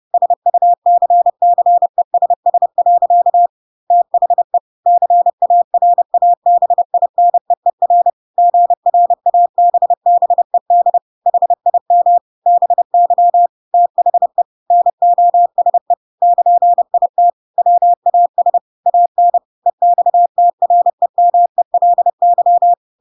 Morse code, heard as a keyed tone, about 30 wpm, 700 Hz. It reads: SUCCESS. THE CARABINEER GRABBED HIM BY THE NOSE <KN>IT WAS AN EXTREMELY
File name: Pinocchio / Pinocchio ch3 30 30 ik1hge